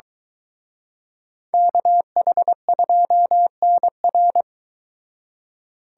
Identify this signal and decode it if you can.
Morse code, heard as a keyed tone, about 23 wpm, 700 Hz. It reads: KH2NR